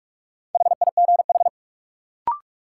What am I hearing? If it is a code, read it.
Morse code, heard as a keyed tone, about 41 wpm, 700 Hz. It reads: HIGH
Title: Sets of 2 Words - New Word Drill - 45wpm